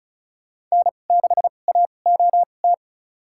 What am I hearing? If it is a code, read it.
Morse code, heard as a keyed tone, about 36 wpm, spaced as for 25 wpm, 700 Hz. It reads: N6AOT
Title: USA Call Signs - 2x Character Spacing - 35wpm